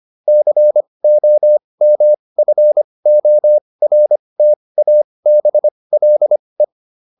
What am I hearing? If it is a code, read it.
Morse code, heard as a keyed tone, about 25 wpm, spaced as for 20 wpm, 600 Hz. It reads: COMFORTABLE